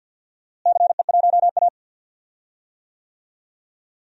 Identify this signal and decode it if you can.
Morse code, heard as a keyed tone, about 47 wpm, 700 Hz. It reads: CE1A
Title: DXCC-100 Call Signs - 4 Characters - Repeated 2x - 50wpm